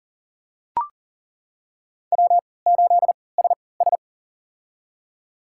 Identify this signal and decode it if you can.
Morse code, heard as a keyed tone, about 38 wpm, spaced as for 22 wpm, 700 Hz. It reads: W8SS
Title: USA Call Signs - 4 Characters - 3x Character Spacing - 40wpm